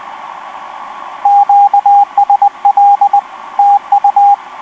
Morse code, heard as a keyed tone, about 20 wpm, 810 Hz, with radio noise. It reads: QSL TU
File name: v5-dk1ce f4ffh etc 10m cw 05jan2011 1458